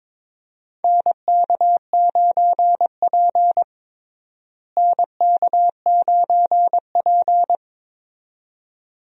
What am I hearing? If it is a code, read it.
Morse code, heard as a keyed tone, about 23 wpm, 700 Hz. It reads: NK9P NK9P
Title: USA Call Signs - 4 Characters - Repeated 2x - 22wpm